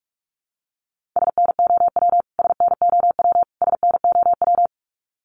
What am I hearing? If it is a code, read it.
Morse code, heard as a keyed tone, about 44 wpm, 700 Hz. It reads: SNOW SNOW SNOW
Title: Top 500 Words Repeated 45wpm